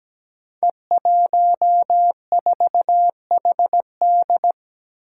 Morse code, heard as a keyed tone, about 17 wpm, 700 Hz. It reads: E14HD
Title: Five Challenging Letter-Number Combinations 17wpm